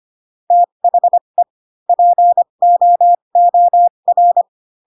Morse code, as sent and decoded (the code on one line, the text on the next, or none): - .... / .--. --- --- .-.
THE POOR